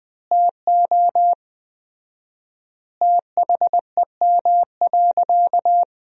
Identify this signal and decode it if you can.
Morse code, heard as a keyed tone, about 20 wpm, 700 Hz. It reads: TO THEM.
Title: 4x Word-Spacing - Sentences from Top 500 Words - 20wpm